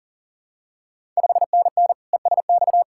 Morse code, heard as a keyed tone, about 43 wpm, 690 Hz. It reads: 5NN ESX
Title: Ontario QSO Party Exchange 40wpm